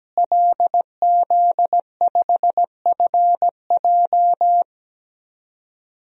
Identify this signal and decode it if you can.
Morse code, heard as a keyed tone, about 17 wpm, 700 Hz. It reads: LZ5FJ
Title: DXCC-100 Call Signs - 5 Characters - Repeated 2x - 17wpm